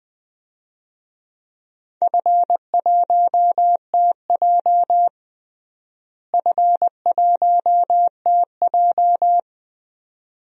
Morse code, heard as a keyed tone, about 20 wpm, 700 Hz. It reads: F1TJ F1TJ
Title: DXCC-100 Call Signs - 4 Characters - Repeated 2x - 20wpm